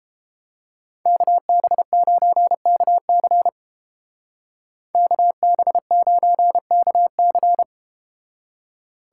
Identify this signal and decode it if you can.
Morse code, heard as a keyed tone, about 31 wpm, 700 Hz. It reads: KB9KC KB9KC